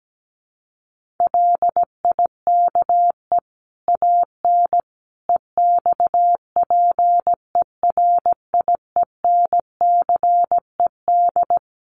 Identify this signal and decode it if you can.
Morse code, heard as a keyed tone, about 17 wpm, 700 Hz. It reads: LIKE AN EXPERIENCED
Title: Sentences from Top 1500 Words 17wpm